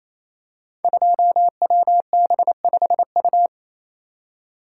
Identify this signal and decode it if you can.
Morse code, heard as a keyed tone, about 28 wpm, 700 Hz. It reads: 2WB5U
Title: Code Groups - 5 Letter:Numbers - 28wpm